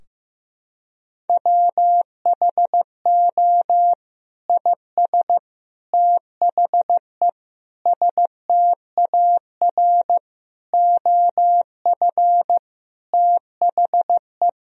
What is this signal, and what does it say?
Morse code, heard as a keyed tone, about 15 wpm, 700 Hz. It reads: WHO IS THE STAR OF THE
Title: Sentences from Top 1000 Words - Review Part 2 - 15wpm